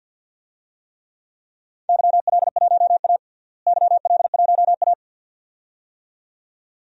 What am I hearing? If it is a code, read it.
Morse code, heard as a keyed tone, about 52 wpm, 700 Hz. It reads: YL1A YL1A
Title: DXCC-100 Call Signs - 4 Characters - Repeated 2x - 50wpm